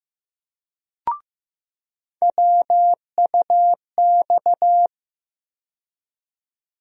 Morse code, heard as a keyed tone, about 15 wpm, 700 Hz. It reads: WUX
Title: USA Call Sign Postfixes - 3 Letters 15wpm